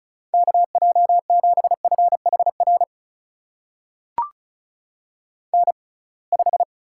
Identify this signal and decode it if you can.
Morse code, heard as a keyed tone, about 36 wpm, spaced as for 27 wpm, 700 Hz. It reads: KJ7FHR N5
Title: USA Call Signs - 6x Character Spacing - 35wpm